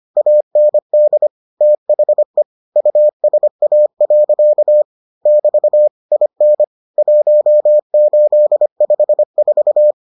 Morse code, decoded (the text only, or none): AND THE USA. <BT> IN 1854